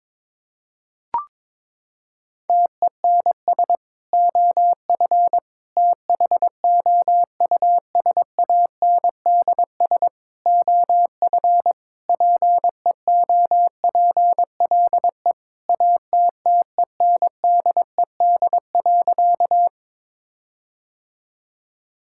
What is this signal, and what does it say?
Morse code, heard as a keyed tone, about 22 wpm, 700 Hz. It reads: TENS OF THOUSANDS OF PEOPLE ATTENDED.
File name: practice / 22wpm / Sentences with 4-Word Phrases - 22wpm